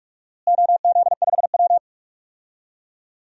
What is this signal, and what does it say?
Morse code, heard as a keyed tone, about 46 wpm, 700 Hz. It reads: OZ5W
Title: DXCC-100 Call Signs - 4 Characters - Repeated 3x - 45wpm